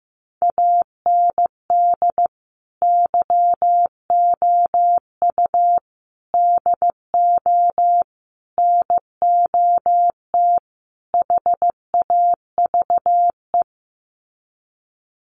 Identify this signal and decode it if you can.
Morse code, heard as a keyed tone, about 14 wpm, 700 Hz. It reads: AND YOU DO NOT HAVE
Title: Sets of 5 Words Encore 15wpm